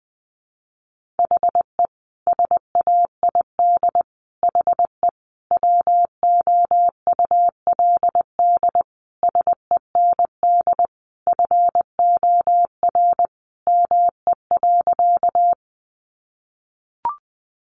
Morse code, heard as a keyed tone, about 20 wpm, 700 Hz. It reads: HE SAID HE WOULD SEND FOR ME.
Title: Sentences from Top 1500 Words 20wpm